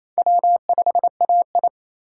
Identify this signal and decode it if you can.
Morse code, heard as a keyed tone, about 34 wpm, 700 Hz. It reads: W5AI
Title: Speed Racing - USA 1x2 Call Signs 28wpm